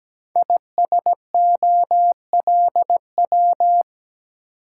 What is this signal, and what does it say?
Morse code, heard as a keyed tone, about 17 wpm, 700 Hz. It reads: ISOLW
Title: Code Groups - 5 Letter:Numbers - 17wpm